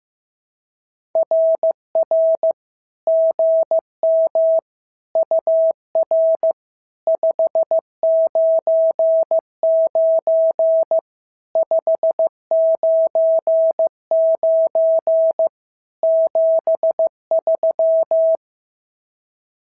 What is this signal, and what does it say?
Morse code, heard as a keyed tone, about 15 wpm, 650 Hz. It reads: RR GM UR 599 599 73
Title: SOTA QSO Practice 15wpm